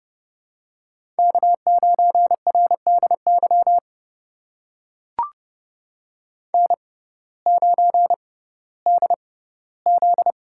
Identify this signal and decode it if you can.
Morse code, heard as a keyed tone, about 30 wpm, spaced as for 17 wpm, 700 Hz. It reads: K9RDY N9DZ
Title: USA Call Signs - 6x Character Spacing - 30wpm